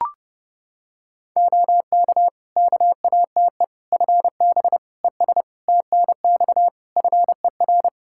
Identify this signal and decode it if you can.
Morse code, heard as a keyed tone, about 30 wpm, 700 Hz. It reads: OK KATE FB ES TNX FER
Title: Practice QSO - Exchange 2 - Info - 30wpm